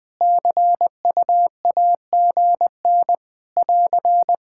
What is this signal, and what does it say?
Morse code, heard as a keyed tone, about 21 wpm, 700 Hz. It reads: CUAGN <AR>
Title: K1USN SST QSO Practice 20wpm